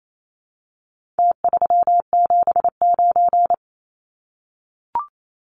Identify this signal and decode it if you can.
Morse code, heard as a keyed tone, about 28 wpm, 700 Hz. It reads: T379
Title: Numbers - Contest Style - 4-Digit 28wpm